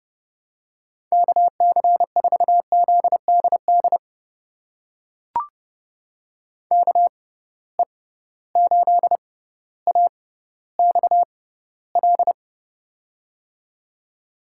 Morse code, heard as a keyed tone, about 30 wpm, spaced as for 15 wpm, 700 Hz. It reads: KC4ZDD KE8AXL
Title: USA Call Signs - 6 Characters - 6x Character Spacing - 30wpm